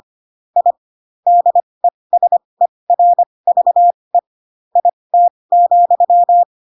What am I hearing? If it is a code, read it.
Morse code, heard as a keyed tone, about 25 wpm, spaced as for 20 wpm, 700 Hz. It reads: I DESERVE IT,